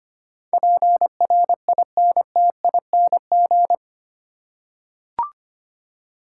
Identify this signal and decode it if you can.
Morse code, heard as a keyed tone, about 25 wpm, 700 Hz. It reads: PRINTING
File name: 8 Letter Words - First 3 Letters Given - 25wpm